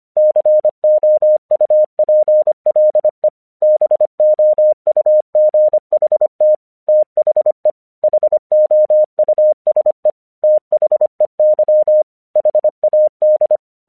Morse code, heard as a keyed tone, about 25 wpm, 610 Hz. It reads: COUPLE BOUGHT THE HOUSE THEY HAD
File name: 1 English 25wpm / AQC05